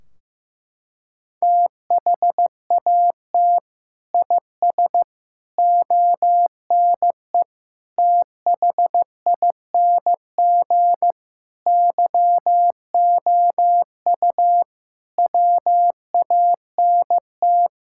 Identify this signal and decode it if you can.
Morse code, heard as a keyed tone, about 15 wpm, 700 Hz. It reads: THAT IS ONE THING YOU WANT
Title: Sentences from Top 1600 Words 15wpm